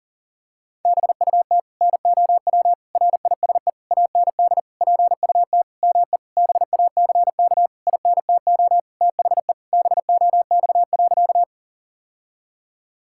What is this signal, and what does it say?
Morse code, heard as a keyed tone, about 40 wpm, 700 Hz. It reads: BUT NOW RISE AND PUT ME BACK INTO THE BOX.
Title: Sentences from Top 1000 Words - Review Part 6 - 40wpm